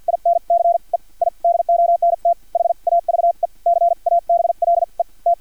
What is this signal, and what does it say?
Morse code, heard as a keyed tone, about 50 wpm, spaced as for 41 wpm, 690 Hz. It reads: INCE I DONT HAVE CABLE T
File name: archived / CRSnet